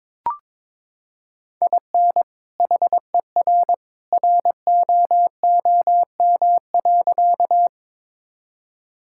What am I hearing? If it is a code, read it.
Morse code, heard as a keyed tone, about 22 wpm, 700 Hz. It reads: IN HER ROOM.
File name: practice / 22wpm / Sentences with 2-Word Phrases - 22wpm